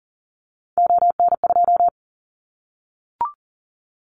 Morse code, heard as a keyed tone, about 40 wpm, 700 Hz. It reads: ON2
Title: Callsign Prefixes for Top 100 DXCC - 40wpm